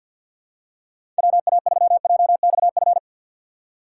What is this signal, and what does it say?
Morse code, heard as a keyed tone, about 49 wpm, 700 Hz. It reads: WA3JXF
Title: USA Call Signs - 3x Character Spacing - 50wpm